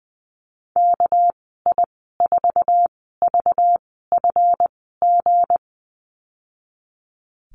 Morse code, heard as a keyed tone, about 20 wpm, spaced as for 14 wpm, 700 Hz. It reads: KI4VFG
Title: 245 Lesson 39g - Letter z - Call Signs 20wpm